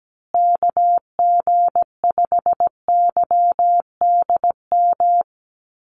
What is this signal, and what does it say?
Morse code, heard as a keyed tone, about 17 wpm, 700 Hz. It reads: KG5YDM
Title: USA Call Signs Repeated 3x 17wpm